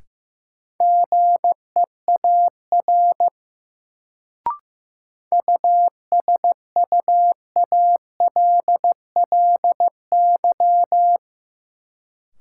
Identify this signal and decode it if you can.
Morse code, heard as a keyed tone, about 15 wpm, 700 Hz. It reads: GEAR USUALLY
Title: Sentences from Top 1400 Words - New Word Drill Alt - 15wpm